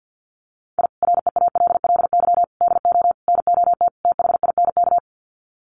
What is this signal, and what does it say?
Morse code, heard as a keyed tone, about 50 wpm, 700 Hz. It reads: I REALLY DO NOT THINK
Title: Sets of 5 Words Encore2 50wpm